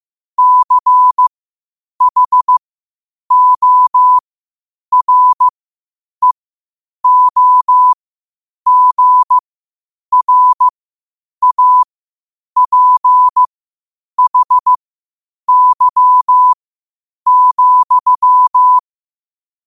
Morse code, heard as a keyed tone, about 15 wpm, spaced as for 8 wpm, 1000 Hz. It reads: CHOREOGRAPHY,
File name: qotdfiles / 8wpm